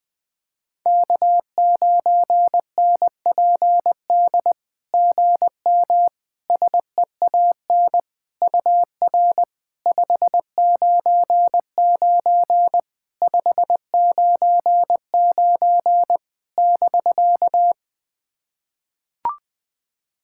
Morse code, heard as a keyed tone, about 21 wpm, 700 Hz. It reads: K9NPD GM SEAN UR 599 599 <BK>
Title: SOTA QSO Practice 20wpm